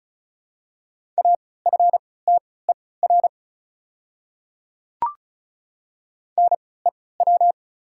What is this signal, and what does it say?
Morse code, heard as a keyed tone, about 36 wpm, spaced as for 20 wpm, 700 Hz. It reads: AFTER NEW
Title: 3x Character Spacing - Top 200 Words - 35wpm